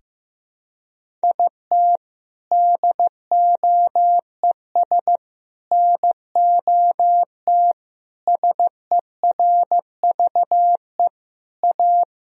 Morse code, heard as a keyed tone, about 15 wpm, 700 Hz. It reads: IT DOES NOT SERVE A
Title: Sentences from Top 1000 Words - Review Part 2 - 15wpm